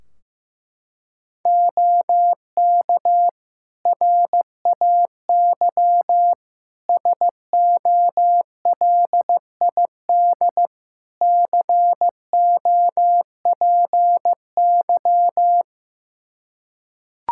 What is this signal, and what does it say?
Morse code, heard as a keyed tone, about 15 wpm, 700 Hz. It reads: OK RAY SOLID COPY
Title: Practice QSO - Exchange 3 - More Info - 15wpm